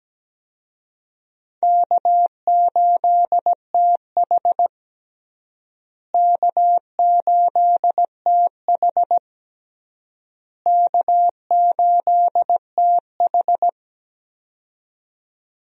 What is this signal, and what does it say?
Morse code, heard as a keyed tone, about 17 wpm, 700 Hz. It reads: K8TH K8TH K8TH
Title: USA Call Signs - 4 Characters - Repeated 3x - 17wpm